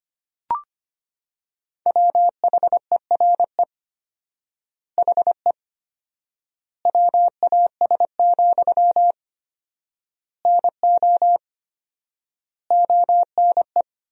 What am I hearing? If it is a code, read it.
Morse code, heard as a keyed tone, about 25 wpm, 700 Hz. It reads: WHERE HE WAS, NO ONE
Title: 4x Word-Spacing - Sentences from Top 400 Words - 25wpm